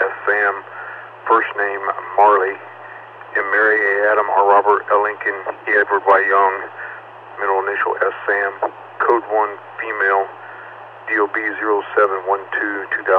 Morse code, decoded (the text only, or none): none